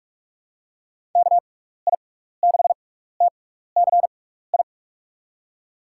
Morse code, heard as a keyed tone, about 43 wpm, spaced as for 15 wpm, 700 Hz. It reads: KI6TCI
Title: USA Call Signs - 6 Characters - 6x Character Spacing - 45wpm